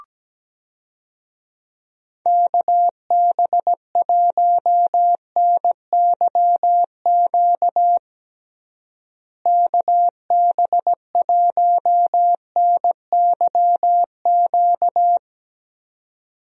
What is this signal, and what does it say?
Morse code, heard as a keyed tone, about 17 wpm, 700 Hz. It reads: KB1NYQ KB1NYQ
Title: USA Call Signs - 6 Characters - Repeated 2x - 17wpm